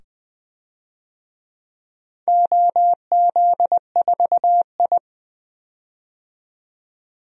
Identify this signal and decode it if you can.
Morse code, heard as a keyed tone, about 20 wpm, 700 Hz. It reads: OZ4I
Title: DXCC-100 Call Signs - 4 Characters - 20wpm